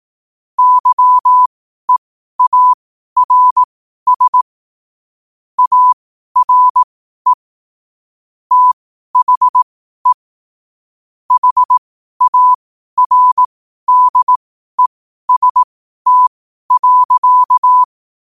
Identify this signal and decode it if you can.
Morse code, heard as a keyed tone, about 18 wpm, spaced as for 12 wpm, 1000 Hz. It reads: YEARS ARE THE HARDEST.
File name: qotdfiles / 12wpm